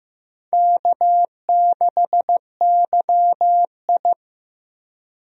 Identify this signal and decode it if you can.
Morse code, heard as a keyed tone, about 15 wpm, 700 Hz. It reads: K6YI
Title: USA Call Signs - 4 Characters - 15wpm